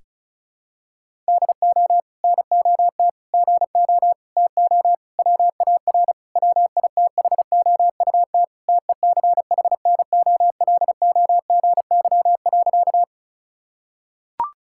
Morse code, heard as a keyed tone, about 36 wpm, 700 Hz. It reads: DO NOT GO TO WAR WITHOUT TECHNOLOGY.
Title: Sentences from Top 1000 Words - Review Part 5 - 35wpm